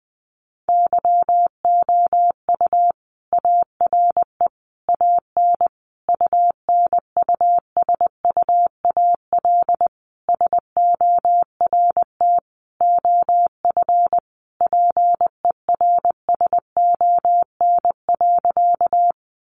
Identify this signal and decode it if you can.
Morse code, heard as a keyed tone, about 20 wpm, 700 Hz. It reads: YOU ARE AN UNUSUAL SORT OF PERSON.
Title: Sentences from Top 1400 Words 20wpm